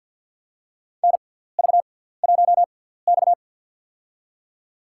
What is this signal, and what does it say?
Morse code, heard as a keyed tone, about 54 wpm, spaced as for 17 wpm, 700 Hz. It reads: NV1X
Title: USA Call Signs - 4 Characters - 6x Character Spacing - 50wpm